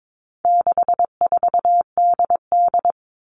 Morse code, heard as a keyed tone, about 22 wpm, 700 Hz. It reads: 64DD
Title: All Cut Numbers - 4-Digit 22wpm